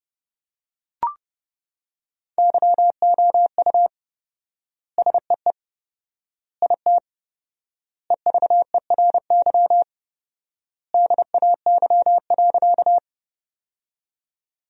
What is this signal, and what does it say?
Morse code, heard as a keyed tone, about 30 wpm, 700 Hz. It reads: YOU SEE IT EVERY DAY.
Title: 4x Word-Spacing - Sentences from Top 200 Words - 30wpm